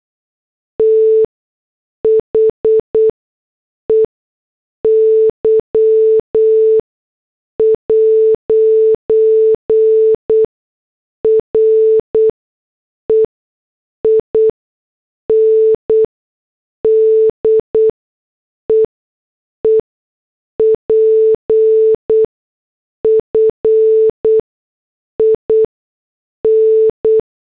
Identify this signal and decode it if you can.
Morse code, heard as a keyed tone, about 8 wpm, spaced as for 6 wpm, 440 Hz. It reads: THEY'RE IN DEEP FIN
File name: morse / headlines